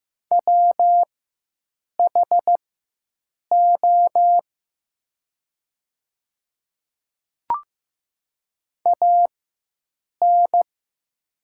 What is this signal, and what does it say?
Morse code, heard as a keyed tone, about 15 wpm, spaced as for 7 wpm, 700 Hz. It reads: WHO AN
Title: 4x Character Spacing - Top 300 Words - 15wpm